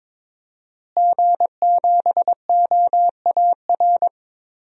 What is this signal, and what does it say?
Morse code, heard as a keyed tone, about 21 wpm, 700 Hz. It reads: G7OAR
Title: DXCC-100 Call Signs - 5 Characters - Repeated 3x - 22wpm